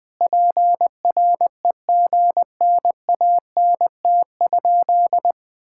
Morse code, heard as a keyed tone, about 20 wpm, 700 Hz. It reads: PREGNANT?
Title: Sentences from Top 1000 Words - Review Part 6 - 20wpm